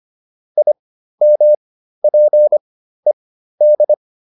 Morse code, heard as a keyed tone, about 25 wpm, spaced as for 13 wpm, 600 Hz. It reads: IMPED